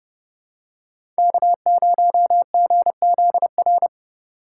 Morse code, heard as a keyed tone, about 30 wpm, 700 Hz. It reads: K0GZR
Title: USA Call Signs - 3x Character Spacing - 30wpm